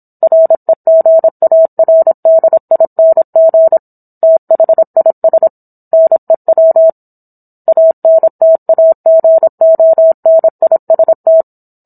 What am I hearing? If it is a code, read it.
Morse code, heard as a keyed tone, about 27 wpm, 650 Hz. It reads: REGARDING THIS NEW ANTAGONIST